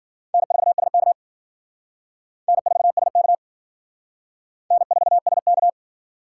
Morse code, heard as a keyed tone, about 54 wpm, 700 Hz. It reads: N4SK N4SK N4SK